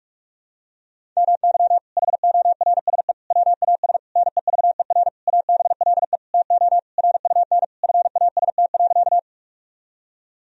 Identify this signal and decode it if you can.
Morse code, heard as a keyed tone, about 46 wpm, 700 Hz. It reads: MY HORSE WAS NEVER ABLE TO RUN FAST.
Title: Sentences from Top 1000 Words - Review Part 2 - 45wpm